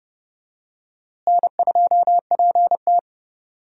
Morse code, heard as a keyed tone, about 31 wpm, 700 Hz. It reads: N2PT
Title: USA Call Signs - 4 Characters - Repeated 2x - 30wpm